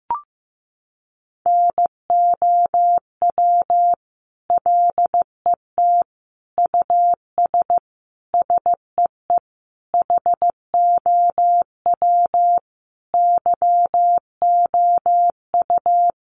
Morse code, heard as a keyed tone, about 15 wpm, 700 Hz. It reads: NOW LET US SEE HOW YOU
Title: Sentences from Top 1400 Words 15wpm